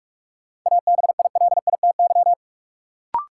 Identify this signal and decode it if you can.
Morse code, heard as a keyed tone, about 46 wpm, 700 Hz. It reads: ABILITY
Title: Sentences from Top 1200 Words - New Word Drill Alt - 45wpm